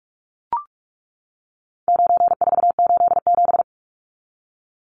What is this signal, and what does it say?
Morse code, heard as a keyed tone, about 44 wpm, 700 Hz. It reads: 9487
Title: Numbers - 4-Digit 45wpm